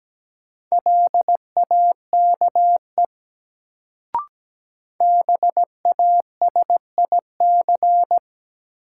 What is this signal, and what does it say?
Morse code, heard as a keyed tone, about 17 wpm, 700 Hz. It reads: LAKE BASIC
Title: Sentences from Top 1000 Words - Review Part 3 - New Word Drill - 17wpm